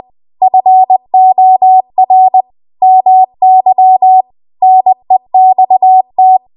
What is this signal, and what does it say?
Morse code, heard as a keyed tone, about 20 wpm, 750 Hz. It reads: FOR MY NEXT
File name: Archive / 20 WPM / 20WPM